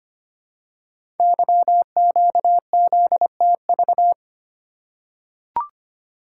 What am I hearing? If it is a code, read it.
Morse code, heard as a keyed tone, about 25 wpm, 700 Hz. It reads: YQZTV
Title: Code Groups - 5 Letter:Numbers - 25wpm